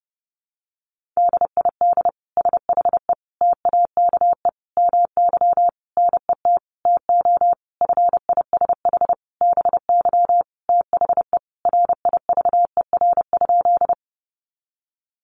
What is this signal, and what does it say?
Morse code, heard as a keyed tone, about 30 wpm, 700 Hz. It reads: DID SHE TAKE MY NET TO FISH BY THE RIVER?
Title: Sentences from Top 1400 Words 30wpm